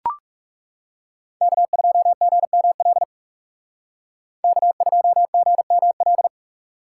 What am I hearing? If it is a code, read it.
Morse code, heard as a keyed tone, about 38 wpm, 700 Hz. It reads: K2GML K2GML